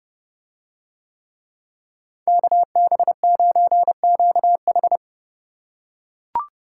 Morse code, heard as a keyed tone, about 29 wpm, 700 Hz. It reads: KB9QH